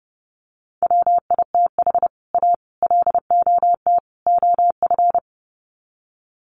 Morse code, heard as a keyed tone, about 28 wpm, 700 Hz. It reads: WITH A LOT OF
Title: Sets of 4 Words 30wpm